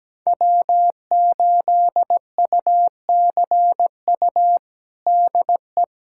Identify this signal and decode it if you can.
Morse code, heard as a keyed tone, about 17 wpm, 700 Hz. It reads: W8UCU DE
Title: USA Callsign DE Callsign 17wpm